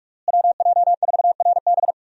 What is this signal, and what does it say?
Morse code, heard as a keyed tone, about 41 wpm, 700 Hz. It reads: WJ4RB
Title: USA Call Signs - 5 Characters - Repeated 3x - 45wpm